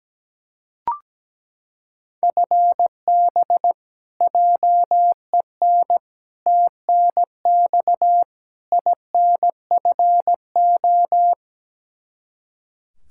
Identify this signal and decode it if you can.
Morse code, heard as a keyed tone, about 17 wpm, 700 Hz. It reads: FB JEN TNX INFO
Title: Practice QSO - Exchange 3 - More Info - 17wpm